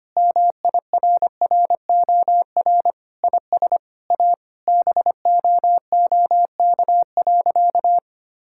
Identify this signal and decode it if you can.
Morse code, heard as a keyed tone, about 25 wpm, 700 Hz. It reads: MIRROR IS A BOOK.
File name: practice / 25wpm / Sentences with 4-Word Phrases - 25wpm